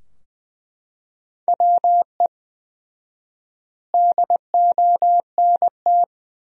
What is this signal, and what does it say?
Morse code, heard as a keyed tone, about 21 wpm, 700 Hz. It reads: WE DONT
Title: 4x Word-Spacing - Sentences from Top 300 Words - 20wpm